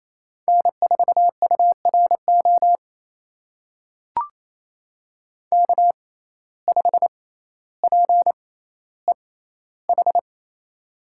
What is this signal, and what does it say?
Morse code, heard as a keyed tone, about 28 wpm, spaced as for 14 wpm, 700 Hz. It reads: N4URO K5PEH
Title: USA Call Signs - 5 Characters - 6x Character Spacing - 28wpm